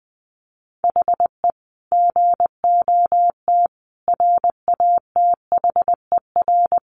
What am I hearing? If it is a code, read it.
Morse code, heard as a keyed tone, about 20 wpm, 700 Hz. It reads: HE GOT RATHER
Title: Sentences from Top 700 Words 20wpm